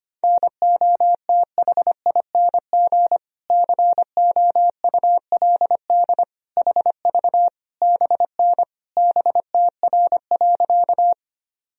Morse code, heard as a keyed tone, about 25 wpm, 700 Hz. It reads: NOTHING COULD HV BN BTR.
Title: Sentences from Top 500 Words and Abbreviations 25wpm